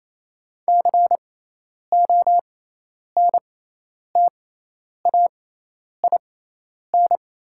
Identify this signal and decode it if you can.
Morse code, heard as a keyed tone, about 28 wpm, spaced as for 10 wpm, 700 Hz. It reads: CONTAIN
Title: 6x Character Spacing - Top 500 Words - 28wpm